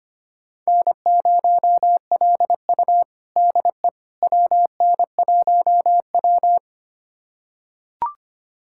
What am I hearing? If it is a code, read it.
Morse code, heard as a keyed tone, about 25 wpm, 700 Hz. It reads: N0LU DE WN1W
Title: USA Callsign DE Callsign - 4 Characters - 25wpm